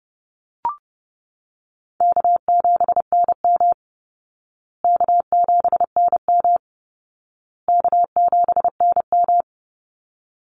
Morse code, heard as a keyed tone, about 30 wpm, 700 Hz. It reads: K7NM K7NM K7NM
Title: Easy Call Signs Repeated 3x - 30wpm